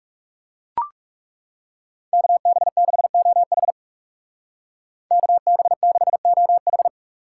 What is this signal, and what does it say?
Morse code, heard as a keyed tone, about 39 wpm, 700 Hz. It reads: KB6OH KB6OH